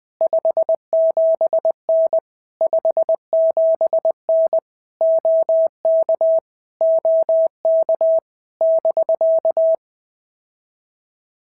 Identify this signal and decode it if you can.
Morse code, heard as a keyed tone, about 20 wpm, 650 Hz. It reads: 57N 57N OK OK <BK>
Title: POTA QSO Practice - 20wpm